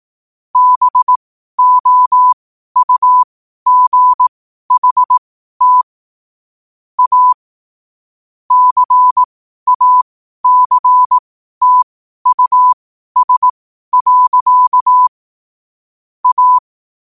Morse code, heard as a keyed tone, about 18 wpm, spaced as for 12 wpm, 1000 Hz. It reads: BOUGHT A CACTUS. A
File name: qotdfiles / 12wpm